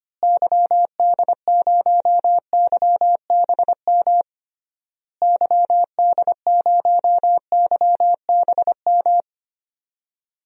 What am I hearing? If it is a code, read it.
Morse code, heard as a keyed tone, about 25 wpm, 700 Hz. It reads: YD0YBM YD0YBM
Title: DXCC-100 Call Signs - 6 Characters - Repeated 3x - 25wpm